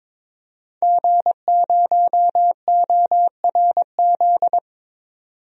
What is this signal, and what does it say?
Morse code, heard as a keyed tone, about 23 wpm, 700 Hz. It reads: G0ORZ
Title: DXCC-100 Call Signs - 5 Characters - 22wpm